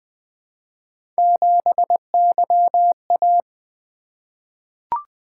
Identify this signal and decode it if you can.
Morse code, heard as a keyed tone, about 20 wpm, 700 Hz. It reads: 7YA
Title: Three Letter-Number Combinations - 20wpm